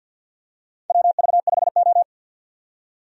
Morse code, heard as a keyed tone, about 47 wpm, 700 Hz. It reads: WV5O
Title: USA Call Signs - 4 Characters - 50wpm